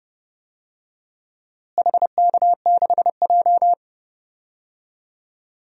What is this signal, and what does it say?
Morse code, heard as a keyed tone, about 32 wpm, 700 Hz. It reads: HK6J
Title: DXCC-100 Call Signs - 4 Characters - Repeated 3x - 30wpm